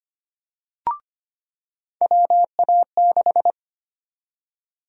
Morse code, heard as a keyed tone, about 25 wpm, 700 Hz. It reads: WA6
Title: USA Call Sign Prefixes - 25wpm